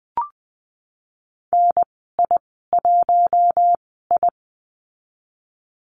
Morse code, heard as a keyed tone, about 20 wpm, spaced as for 15 wpm, 700 Hz. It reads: NI1I
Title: 055 Lesson 10g - Letter r - Call Signs 20wpm